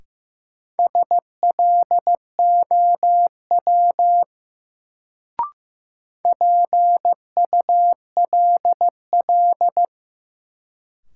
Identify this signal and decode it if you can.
Morse code, heard as a keyed tone, about 15 wpm, 700 Hz. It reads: SLOW PULL
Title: Sentences from Top 400 Words - New Word Drill - 15wpm